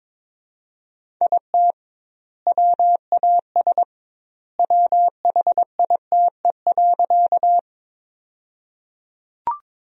Morse code, heard as a keyed tone, about 22 wpm, 700 Hz. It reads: IT WAS WHITE.
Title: 2x Word-Spacing - Sentences from Top 300 Words - 22wpm